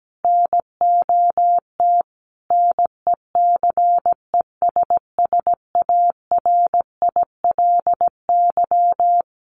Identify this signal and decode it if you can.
Morse code, heard as a keyed tone, about 17 wpm, 700 Hz. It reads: NOT NECESSARILY
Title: Sentences from Top 1400 Words 17wpm